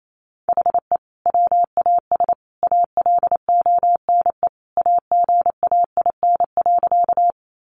Morse code, heard as a keyed tone, about 28 wpm, 700 Hz. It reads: HE WAS ALONE AGAIN.
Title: Sentences from Top 1000 Words 28wpm